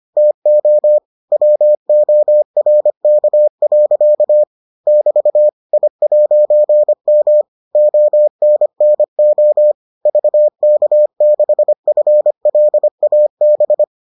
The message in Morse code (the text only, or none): TO WORK. <BT> I'M ONNO VK6FLAB